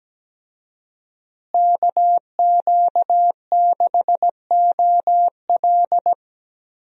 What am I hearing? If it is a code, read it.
Morse code, heard as a keyed tone, about 17 wpm, 700 Hz. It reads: KQ6OL
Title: USA Call Signs - 5 Characters - Repeated 3x - 17wpm